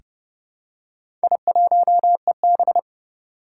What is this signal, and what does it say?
Morse code, heard as a keyed tone, about 30 wpm, 700 Hz. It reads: I1EB
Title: Four Challenging Letter-Number Combinations 30wpm